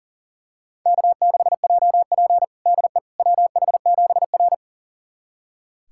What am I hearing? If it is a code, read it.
Morse code, heard as a keyed tone, about 40 wpm, 700 Hz. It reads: K6JP DE WH7R
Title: USA Callsign DE Callsign - 4 Characters - 40wpm